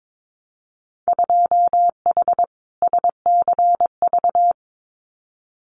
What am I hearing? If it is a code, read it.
Morse code, heard as a keyed tone, about 22 wpm, 700 Hz. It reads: 2H SCV
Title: Winter Field Day Exchanges Encore 22wpm